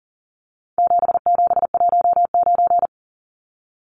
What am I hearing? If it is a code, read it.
Morse code, heard as a keyed tone, about 40 wpm, 700 Hz. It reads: 7719
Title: Numbers - 4-Digit 40wpm